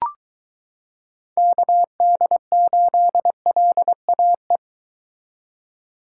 Morse code, heard as a keyed tone, about 23 wpm, 700 Hz. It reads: KD8LAE